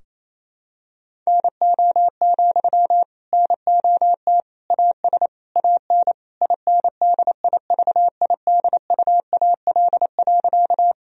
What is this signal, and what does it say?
Morse code, heard as a keyed tone, about 28 wpm, 700 Hz. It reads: NO, NOT AS AN INDIVIDUAL.
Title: Sentences from Top 1000 Words - Review Part 4 - 28wpm